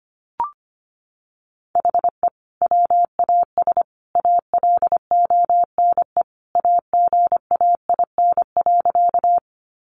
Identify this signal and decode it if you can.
Morse code, heard as a keyed tone, about 25 wpm, 700 Hz. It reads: HE WAS ALONE AGAIN.
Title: Sentences from Top 1000 Words 25wpm